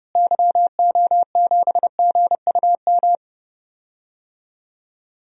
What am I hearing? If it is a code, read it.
Morse code, heard as a keyed tone, about 30 wpm, 700 Hz. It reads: YO7GUM
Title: DXCC-100 Call Signs - 6 Characters - Repeated 2x - 30wpm